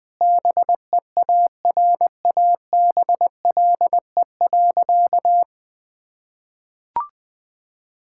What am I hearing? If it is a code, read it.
Morse code, heard as a keyed tone, about 20 wpm, 700 Hz. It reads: BEARABLE.
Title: Difficult Sentences with Hints - 20wpm